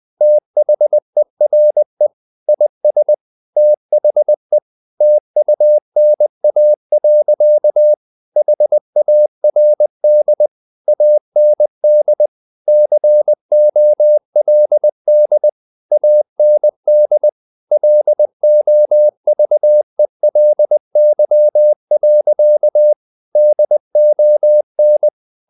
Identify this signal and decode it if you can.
Morse code, heard as a keyed tone, about 20 wpm, 600 Hz. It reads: THERE IS THE TUNA. HARD AND COLD AND LOVELY. DON
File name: the old man and the sea / the old man and the sea part 1 20wpm